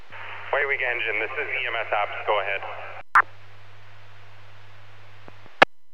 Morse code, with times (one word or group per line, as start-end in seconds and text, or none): none